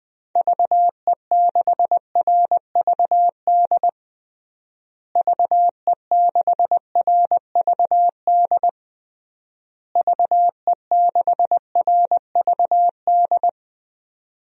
Morse code, heard as a keyed tone, about 21 wpm, 700 Hz. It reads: VE6RVD VE6RVD VE6RVD
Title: DXCC-100 Call Signs - 6 Characters - Repeated 3x - 20wpm